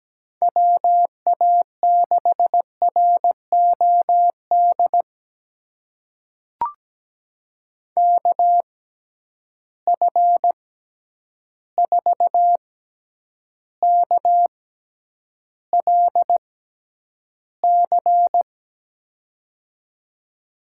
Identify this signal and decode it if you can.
Morse code, heard as a keyed tone, about 17 wpm, spaced as for 9 wpm, 700 Hz. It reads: WA6ROD KF4KLC
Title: USA Call Signs - 6 Characters - 6x Character Spacing - 17wpm